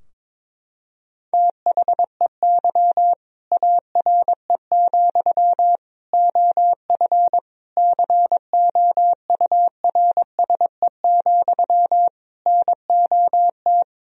Morse code, heard as a keyed tone, about 22 wpm, 700 Hz. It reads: THEY ARE, OF COURSE, NOT
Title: Sentences from Top 1000 Words - Review Part 4 - 22wpm